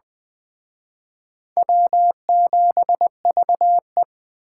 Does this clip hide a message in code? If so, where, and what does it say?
Yes: Morse code from 1.6 to 4.0 s, heard as a keyed tone, about 20 wpm, 700 Hz. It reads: W7VE